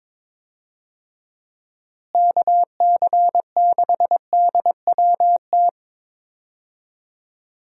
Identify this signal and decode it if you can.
Morse code, heard as a keyed tone, about 22 wpm, 700 Hz. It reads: KC6DWT
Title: USA Call Signs - 6 Characters - Repeated 2x - 22wpm